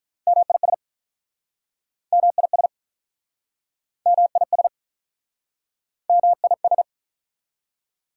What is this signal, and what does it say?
Morse code, heard as a keyed tone, about 47 wpm, 690 Hz. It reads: MIS MIS MIS MIS